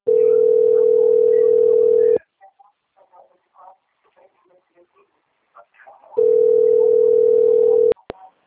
Ringback tone (North America), 440 + 480 Hz.
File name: out- - - - - .28492